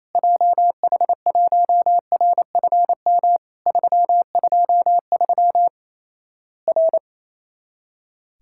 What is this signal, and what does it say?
Morse code, heard as a keyed tone, about 28 wpm, 700 Hz. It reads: JH1RFM 323 R
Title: Worked All Europe DX Contest - QTC Exchange - 28wpm